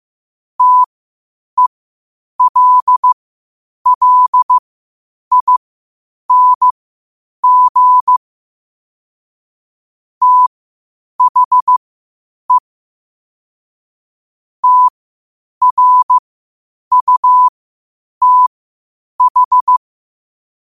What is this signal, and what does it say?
Morse code, heard as a keyed tone, about 15 wpm, spaced as for 8 wpm, 1000 Hz. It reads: TELLING THE TRUTH
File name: qotdfiles / 8wpm